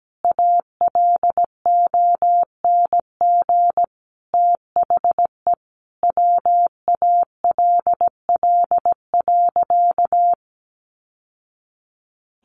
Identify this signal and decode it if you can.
Morse code, heard as a keyed tone, about 17 wpm, 700 Hz. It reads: ALONG THE WALL.
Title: Sentences from Top 1100 Words 17wpm